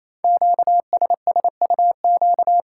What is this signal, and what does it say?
Morse code, heard as a keyed tone, about 28 wpm, 700 Hz. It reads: QSSUQ
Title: Code Groups - 5 Letters - 28wpm